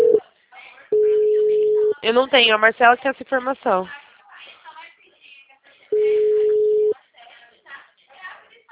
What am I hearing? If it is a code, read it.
Ringback tone (Europe, 425 Hz), with speech.